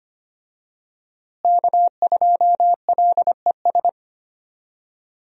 Morse code, heard as a keyed tone, about 25 wpm, 700 Hz. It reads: K2LES
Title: USA Call Signs - 5 Characters - Repeated 2x - 25wpm